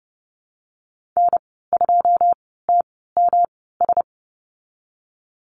Morse code, heard as a keyed tone, about 30 wpm, spaced as for 17 wpm, 700 Hz. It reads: N2TMS
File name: Lesson 19g - Letter w - Call Signs 30wpm